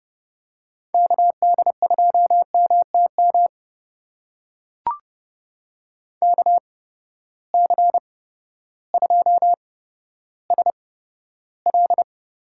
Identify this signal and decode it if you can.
Morse code, heard as a keyed tone, about 30 wpm, spaced as for 14 wpm, 700 Hz. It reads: KD2MTM KC2SL
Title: USA Call Signs - 6 Characters - 8x Character Spacing - 30wpm